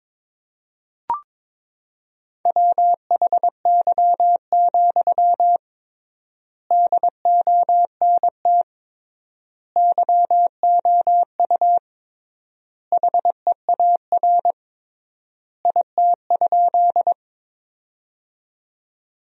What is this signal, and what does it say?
Morse code, heard as a keyed tone, about 21 wpm, 700 Hz. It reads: WHY, DONT YOU HEAR IT?
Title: 3x Word-Spacing - Sentences from Top 300 Words - 22wpm